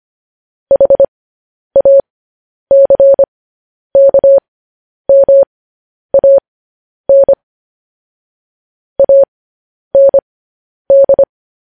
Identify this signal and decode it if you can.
Morse code, heard as a keyed tone, about 25 wpm, spaced as for 10 wpm, 550 Hz. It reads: HACKMAN AND
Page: News Headlines in Morse Code at 10 WPM